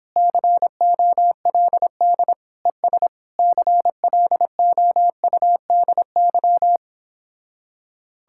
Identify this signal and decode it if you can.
Morse code, heard as a keyed tone, about 31 wpm, 700 Hz. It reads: COLD ES CLOUDY